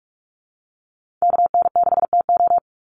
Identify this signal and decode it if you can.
Morse code, heard as a keyed tone, about 44 wpm, 700 Hz. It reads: KN6TO
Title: USA Call Signs 45wpm